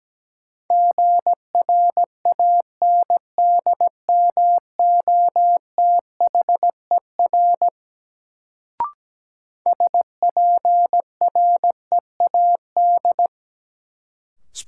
Morse code, heard as a keyed tone, about 17 wpm, 700 Hz. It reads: GRANDMOTHER SPREAD
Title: Sentences from Top 1200 Words - New Word Drill - 17wpm